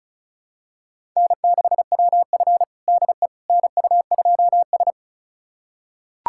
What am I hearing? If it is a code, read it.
Morse code, heard as a keyed tone, about 34 wpm, 700 Hz. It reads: N6WF DE NU2S
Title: USA Callsign DE Callsign - 4 Characters - 35wpm